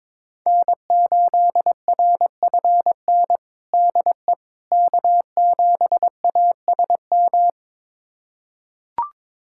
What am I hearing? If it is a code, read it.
Morse code, heard as a keyed tone, about 22 wpm, 700 Hz. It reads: N8RFN DE K7ASM
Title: USA Callsign DE Callsign - 5 Characters - 22wpm